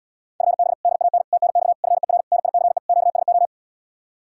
Morse code, heard as a keyed tone, about 80 wpm, 700 Hz. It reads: FB DAN TNX FER NICE QSO